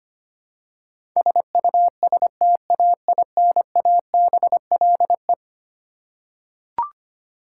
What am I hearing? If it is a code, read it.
Morse code, heard as a keyed tone, about 25 wpm, 700 Hz. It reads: SUSTAINABLE
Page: morse-code-ninja-practice-25wpm